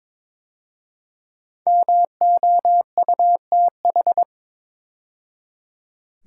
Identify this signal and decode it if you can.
Morse code, heard as a keyed tone, about 22 wpm, 700 Hz. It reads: MOUTH
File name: 5 Letter Words - First 3 Letters Given - 22wpm